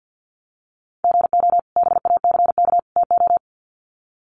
Morse code, heard as a keyed tone, about 50 wpm, 700 Hz. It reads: GO BACK TO
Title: Sets of 3 Words 50wpm